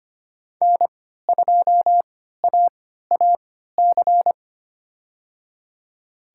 Morse code, heard as a keyed tone, about 25 wpm, spaced as for 14 wpm, 700 Hz. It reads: N2AAC
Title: USA Call Signs - 5 Characters - 3x Character Spacing - 25wpm